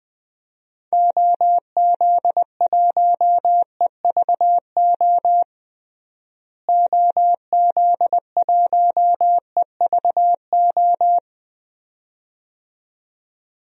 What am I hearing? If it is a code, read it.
Morse code, heard as a keyed tone, about 20 wpm, 700 Hz. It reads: OZ1EVO OZ1EVO
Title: DXCC-100 Call Signs - 6 Characters - Repeated 2x - 20wpm